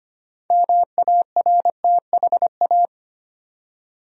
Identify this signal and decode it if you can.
Morse code, heard as a keyed tone, about 25 wpm, 700 Hz. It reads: MARTHA
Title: USA Common Names - 25wpm